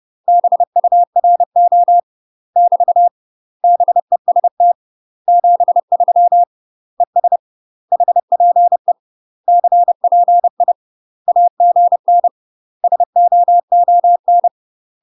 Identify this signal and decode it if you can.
Morse code, heard as a keyed tone, about 30 wpm, 700 Hz. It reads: BURO <BT> BEST 73 ES HPE CPI AGN SOON
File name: UK MASTER QSOs TRAINING / 30WPM / UK MASTER QSOs 0002